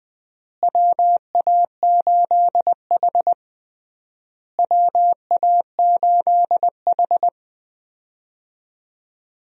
Morse code, heard as a keyed tone, about 20 wpm, 700 Hz. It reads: WA8H WA8H
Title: USA Call Signs - 4 Characters - Repeated 3x - 20wpm